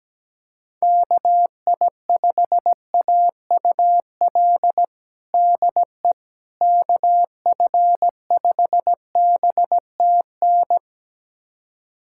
Morse code, heard as a keyed tone, about 17 wpm, 700 Hz. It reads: KI5AUL DE KF5BTN
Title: USA Callsign DE Callsign - 6 Characters - 17wpm